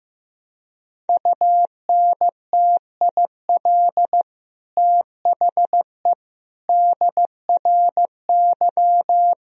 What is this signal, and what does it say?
Morse code, heard as a keyed tone, about 16 wpm, 700 Hz. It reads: UNTIL THE DRY
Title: Sentences from Top 1000 Words - Review Part 3 - 15wpm